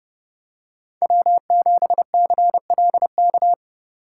Morse code, heard as a keyed tone, about 28 wpm, 700 Hz. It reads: W7CLK